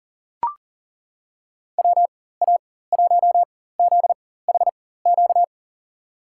Morse code, heard as a keyed tone, about 40 wpm, spaced as for 19 wpm, 700 Hz. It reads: WA1ZHQ
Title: USA Call Signs - 6 Characters - 4x Character Spacing - 40wpm